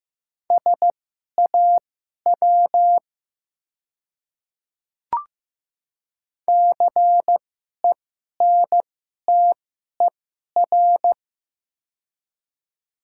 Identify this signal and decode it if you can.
Morse code, heard as a keyed tone, about 15 wpm, spaced as for 11 wpm, 700 Hz. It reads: SAW CENTER
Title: 2x Character Spacing - Top 500 Words - 15wpm